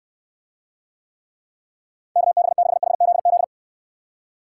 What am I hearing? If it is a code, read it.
Morse code, heard as a keyed tone, about 69 wpm, 700 Hz. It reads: KB6HZZ